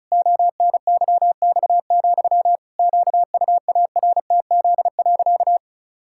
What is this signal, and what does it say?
Morse code, heard as a keyed tone, about 36 wpm, 700 Hz. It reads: ONYX, QUARTZ.
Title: Pangram Challenge v2 - 35wpm